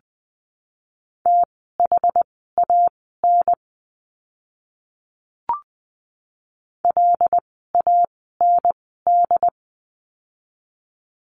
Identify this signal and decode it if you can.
Morse code, heard as a keyed tone, about 20 wpm, spaced as for 14 wpm, 700 Hz. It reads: THAN LAND
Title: 101 Lesson 17f - Letter u - Words 20wpm